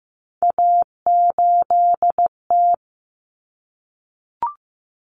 Morse code, heard as a keyed tone, about 15 wpm, 700 Hz. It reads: A8T
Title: Common Cut Numbers - 3-Digit 15wpm